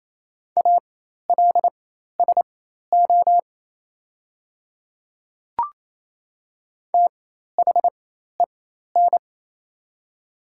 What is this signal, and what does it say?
Morse code, heard as a keyed tone, about 28 wpm, spaced as for 13 wpm, 700 Hz. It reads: ALSO THEN
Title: 4x Character Spacing - Top 200 Words - 28wpm